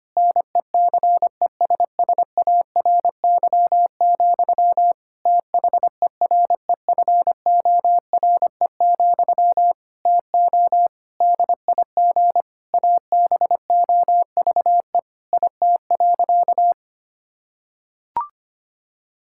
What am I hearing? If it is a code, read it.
Morse code, heard as a keyed tone, about 25 wpm, 700 Hz. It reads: NECESSARY, THEREFORE, TO DIG ABOVE IT.
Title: Sentences from Top 1600 Words 25wpm